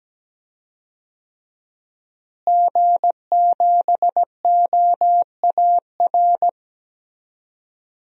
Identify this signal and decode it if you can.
Morse code, heard as a keyed tone, about 17 wpm, 700 Hz. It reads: G7OAR
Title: DXCC-100 Call Signs - 5 Characters - Repeated 3x - 17wpm